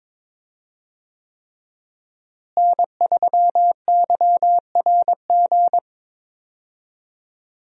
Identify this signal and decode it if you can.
Morse code, heard as a keyed tone, about 22 wpm, 700 Hz. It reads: N3YRG
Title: USA Call Signs - 5 Characters - Repeated 3x - 22wpm